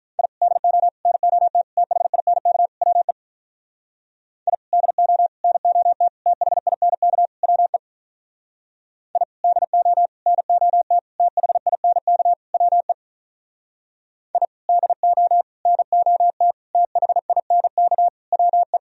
Morse code, heard as a keyed tone, about 44 wpm, 700 Hz. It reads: I DO NOT THINK WE I DO NOT THINK WE I DO NOT THINK WE I DO NOT THINK WE